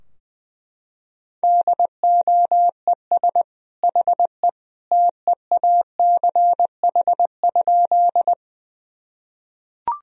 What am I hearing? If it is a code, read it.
Morse code, heard as a keyed tone, about 21 wpm, 700 Hz. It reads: DOES HE TEACH?
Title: Sentences from Top 1000 Words - Review Part 2 - 20wpm